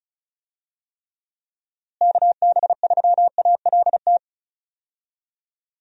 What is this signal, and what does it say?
Morse code, heard as a keyed tone, about 36 wpm, 700 Hz. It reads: KB3ALT
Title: USA Call Signs - 6 Characters - Repeated 3x - 35wpm